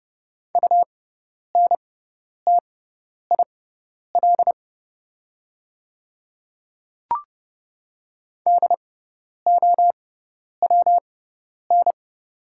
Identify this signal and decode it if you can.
Morse code, heard as a keyed tone, about 30 wpm, spaced as for 10 wpm, 700 Hz. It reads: UNTIL DOWN
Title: 6x Character Spacing - Top 500 Words - 30wpm